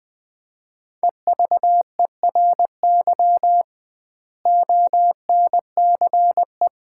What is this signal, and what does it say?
Morse code, heard as a keyed tone, about 20 wpm, 700 Hz. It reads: EVERY ONCE
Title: 2x Word-Spacing - Sets of 5 Words - 20wpm